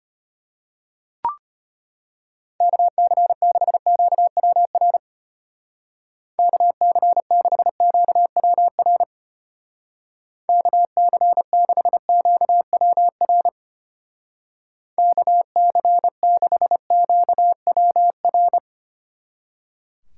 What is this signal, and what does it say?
Morse code, heard as a keyed tone, about 32 wpm, 700 Hz. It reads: KC6QWR KC6QWR KC6QWR KC6QWR